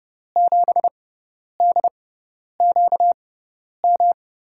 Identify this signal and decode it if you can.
Morse code, heard as a keyed tone, about 30 wpm, spaced as for 10 wpm, 700 Hz. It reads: 7DQM
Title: USA Call Signs - 6 Characters - 6x Character Spacing - 30wpm